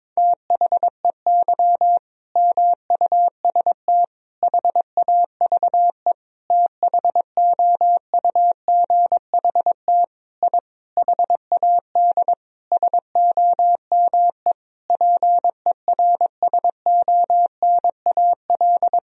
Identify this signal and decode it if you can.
Morse code, heard as a keyed tone, about 22 wpm, 700 Hz. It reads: THEY MUST HAVE THOUGHT I HAD SOME PERSONAL
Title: Sentences from Top 1000 Words - Review Part 4 - 22wpm